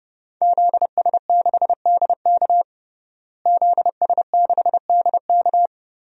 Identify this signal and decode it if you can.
Morse code, heard as a keyed tone, about 30 wpm, 700 Hz. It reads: ZS6DK ZS6DK
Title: DXCC-100 Call Signs - 5 Characters - Repeated 2x - 30wpm